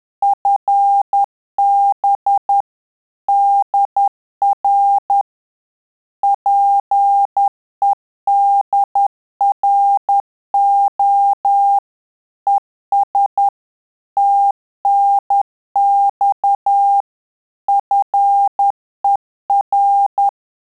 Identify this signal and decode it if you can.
Morse code, heard as a keyed tone, about 10 wpm, 800 Hz. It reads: FB DR PEDRO ES TNX FER
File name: Telegraphy-group / QSO CW 50 CPM